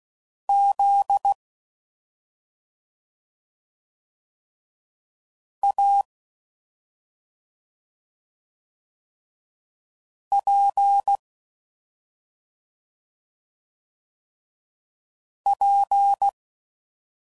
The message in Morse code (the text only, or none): Z A P P